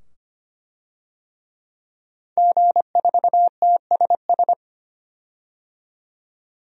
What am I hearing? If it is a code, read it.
Morse code, heard as a keyed tone, about 25 wpm, 700 Hz. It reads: G4TSS
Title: DXCC-100 Call Signs - 5 Characters - 25wpm